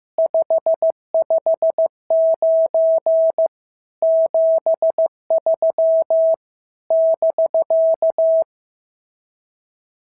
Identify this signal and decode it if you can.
Morse code, heard as a keyed tone, about 15 wpm, 650 Hz. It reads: 559 73 <BK>
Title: SOTA QSO Practice 15wpm